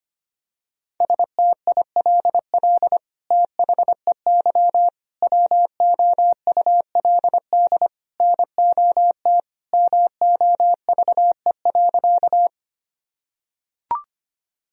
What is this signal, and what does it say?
Morse code, heard as a keyed tone, about 25 wpm, 700 Hz. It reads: STILL THEY WOULD NOT MOVE.
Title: Sentences from Top 1000 Words - Review Part 1 - 25wpm